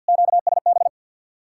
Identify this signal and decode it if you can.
Morse code, heard as a keyed tone, about 64 wpm, 700 Hz. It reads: QSB